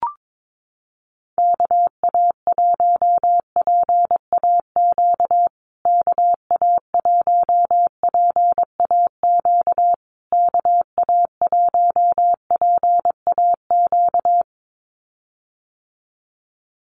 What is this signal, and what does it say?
Morse code, heard as a keyed tone, about 22 wpm, 700 Hz. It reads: KA1PAQ KA1PAQ KA1PAQ
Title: USA Call Signs Repeated 3x 22wpm